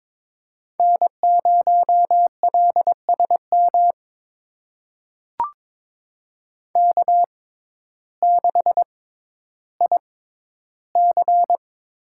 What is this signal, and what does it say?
Morse code, heard as a keyed tone, about 22 wpm, spaced as for 12 wpm, 700 Hz. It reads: N0LSM K6IC
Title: USA Call Signs - 6x Character Spacing - 22wpm